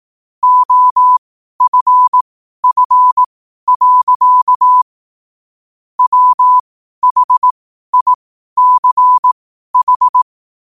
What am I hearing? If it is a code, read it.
Morse code, heard as a keyed tone, about 18 wpm, spaced as for 12 wpm, 1000 Hz. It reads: OFF. WHICH
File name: qotdfiles / 12wpm